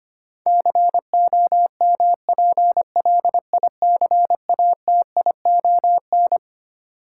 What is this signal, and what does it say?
Morse code, heard as a keyed tone, about 25 wpm, 700 Hz. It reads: COMPLICATION
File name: Twelve Letter Words - 25wpm